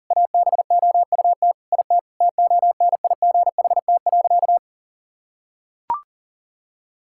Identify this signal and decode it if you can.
Morse code, heard as a keyed tone, about 37 wpm, 700 Hz. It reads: ABOUT IT TONIGHT.
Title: Sentences from Top 1000 Words - Review Part 5 - 40wpm